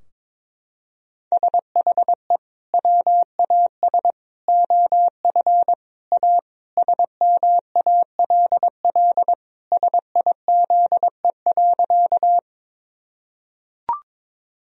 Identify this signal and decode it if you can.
Morse code, heard as a keyed tone, about 21 wpm, 700 Hz. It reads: SHE WAS OF A SMALL SIZE.
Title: Sentences from Top 1000 Words - Review Part 2 - 22wpm